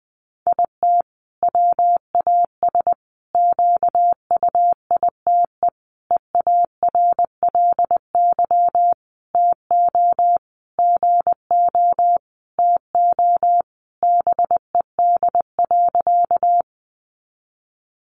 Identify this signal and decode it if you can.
Morse code, heard as a keyed tone, about 20 wpm, 700 Hz. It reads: IT WAS QUITE EARLY TO GO TO BED.
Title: Sentences from Top 700 Words 20wpm